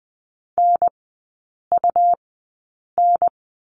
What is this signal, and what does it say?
Morse code, heard as a keyed tone, about 20 wpm, spaced as for 8 wpm, 700 Hz. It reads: NUN
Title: 121 Lesson 20e - Number 3 - Three Characters 20wpm